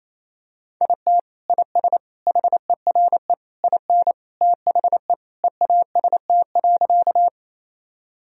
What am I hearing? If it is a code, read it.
Morse code, heard as a keyed tone, about 28 wpm, 700 Hz. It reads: IT IS HERE IN THE EAST.
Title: Sentences from Top 1000 Words - Review Part 3 - 28wpm